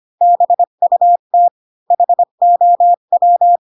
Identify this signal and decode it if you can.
Morse code, heard as a keyed tone, about 25 wpm, 700 Hz. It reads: BUT HOW
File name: Pinocchio / Pinocchio ch8 20 25 ik1hge